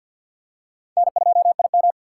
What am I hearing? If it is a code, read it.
Morse code, heard as a keyed tone, about 49 wpm, 700 Hz. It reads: N2IM
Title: USA Call Signs - 4x Character Spacing - 50wpm